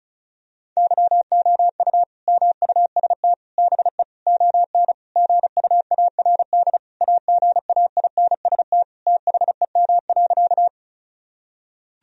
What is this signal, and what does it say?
Morse code, heard as a keyed tone, about 36 wpm, 700 Hz. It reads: YOU MUST BE ON GUARD AGAINST THEM.
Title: Sentences from Top 1000 Words - Review Part 4 - 35wpm